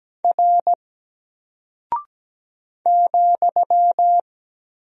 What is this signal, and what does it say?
Morse code, heard as a keyed tone, about 17 wpm, 700 Hz. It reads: R ,
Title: Single Character (Letter:Number:Punctuation:Prosign) 17wpm